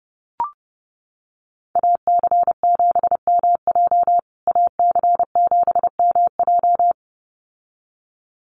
Morse code, heard as a keyed tone, about 30 wpm, 700 Hz. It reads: AC7MJ AC7MJ
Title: USA Call Signs Repeated 2x 30wpm